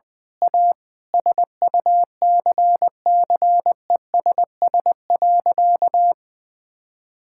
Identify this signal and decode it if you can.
Morse code, heard as a keyed tone, about 19 wpm, 700 Hz. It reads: A SUCCESS.
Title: Sentences with 9-Word Phrases - 20wpm